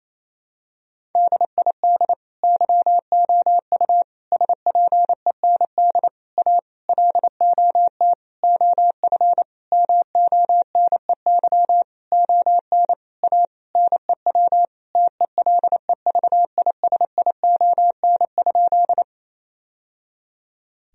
Morse code, heard as a keyed tone, about 27 wpm, 700 Hz. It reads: DID YOU SPEND A LOT OF MONEY ON A NEW TELEVISION?
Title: Sentences from Top 1000 Words - Review Part 4 - 28wpm